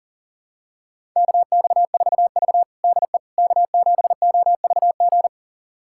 Morse code, heard as a keyed tone, about 40 wpm, 700 Hz. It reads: KX4V DE K7OVG
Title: USA Callsign DE Callsign 40wpm